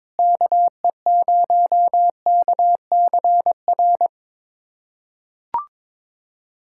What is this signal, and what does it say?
Morse code, heard as a keyed tone, about 22 wpm, 700 Hz. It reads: KE0KCR
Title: USA Callsign DE Callsign - 6 Characters - 22wpm